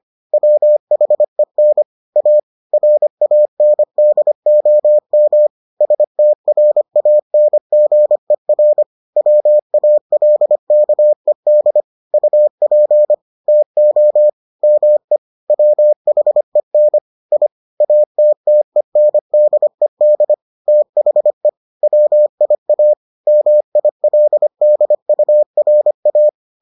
Morse code, heard as a keyed tone, about 25 wpm, 600 Hz. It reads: WHEN A RANDOM STRANGER WALKED UP TO ME WHEN I ATTENDED THE WIA MILDURA